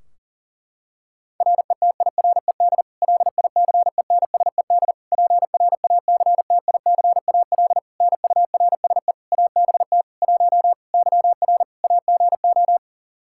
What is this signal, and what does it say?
Morse code, heard as a keyed tone, about 40 wpm, 700 Hz. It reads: RETIRED LICENSED PRACTICAL NURSE ABT 1 YR AGO
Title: Practice QSO - Exchange 3 - More Info - 40wpm